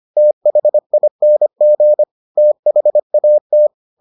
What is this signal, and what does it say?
Morse code, heard as a keyed tone, about 25 wpm, 600 Hz. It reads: THING THAT